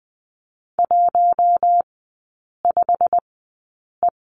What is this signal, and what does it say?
Morse code, heard as a keyed tone, about 20 wpm, spaced as for 8 wpm, 700 Hz. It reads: 15E
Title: 087 Lesson 15e - Number 5 - Three Characters 20wpm